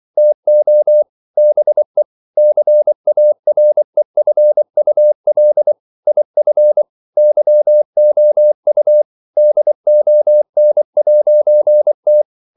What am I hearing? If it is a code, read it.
Morse code, heard as a keyed tone, about 24 wpm, 600 Hz. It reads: TO BE CAREFUL IF YOU DON'T
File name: CW / Practice / EnglishQuotations0324